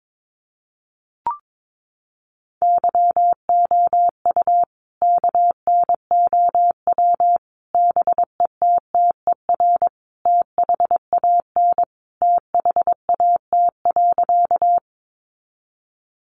Morse code, heard as a keyed tone, about 22 wpm, 700 Hz. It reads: YOU KNOW BETTER THAN THAT.
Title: Sentences from Top 400 Words 22wpm